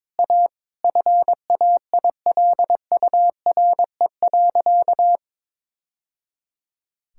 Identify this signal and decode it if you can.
Morse code, heard as a keyed tone, about 22 wpm, 700 Hz. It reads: A FAILURE.
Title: Sentences from Top 1000 Words - Review Part 6 - 22wpm